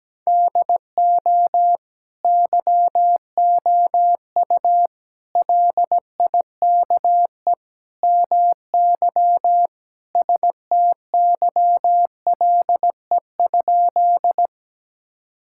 Morse code, heard as a keyed tone, about 17 wpm, 700 Hz. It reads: DO YOU LIKE MY STYLE?
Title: Sentences from Top 1000 Words - Review Part 5 - 17wpm